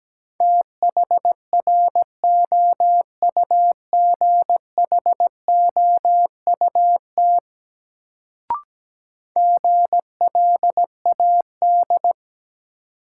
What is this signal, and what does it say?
Morse code, heard as a keyed tone, about 17 wpm, 700 Hz. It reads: THROUGHOUT GLAD
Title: Sentences from Top 800 Words - New Word Drill - 17wpm